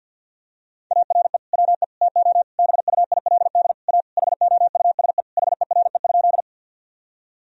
Morse code, heard as a keyed tone, about 51 wpm, 700 Hz. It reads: ARE WE TO BUILD A HOUSE HERE?
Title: Sentences from Top 1000 Words - Review Part 1 - 50wpm